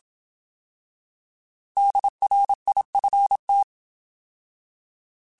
Morse code, heard as a keyed tone, about 27 wpm, 800 Hz. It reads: DRIFT